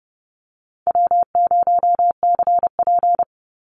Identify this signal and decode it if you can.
Morse code, heard as a keyed tone, about 30 wpm, 700 Hz. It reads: W0CP
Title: NA SOTA Activator Callsigns 30wpm